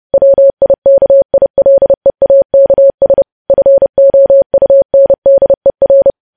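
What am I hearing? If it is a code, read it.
Morse code, heard as a keyed tone, about 30 wpm, 550 Hz. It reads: WIKILEAKS FOUNDER
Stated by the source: News headlines sent as Morse practice at 30.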